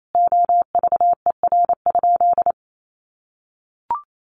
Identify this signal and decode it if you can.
Morse code, heard as a keyed tone, about 28 wpm, 700 Hz. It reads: OVER?
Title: Sentences from Top 1200 Words 28wpm